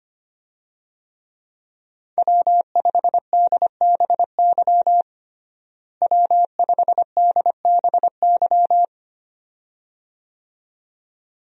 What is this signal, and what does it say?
Morse code, heard as a keyed tone, about 25 wpm, 700 Hz. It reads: W5DBY W5DBY
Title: USA Call Signs - 5 Characters - Repeated 2x - 25wpm